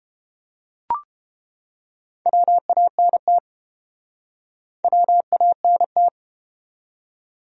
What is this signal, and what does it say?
Morse code, heard as a keyed tone, about 32 wpm, 700 Hz. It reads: WANT WANT